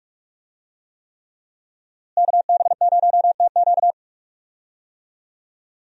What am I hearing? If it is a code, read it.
Morse code, heard as a keyed tone, about 46 wpm, 700 Hz. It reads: KB0TQ
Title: USA Call Signs - 5 Characters - Repeated 3x - 45wpm